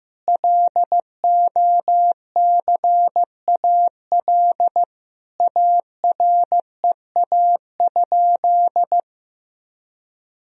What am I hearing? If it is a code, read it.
Morse code, heard as a keyed tone, about 15 wpm, 700 Hz. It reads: LOCAL AREA?
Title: Sentences from Top 1000 Words - Review Part 3 - 15wpm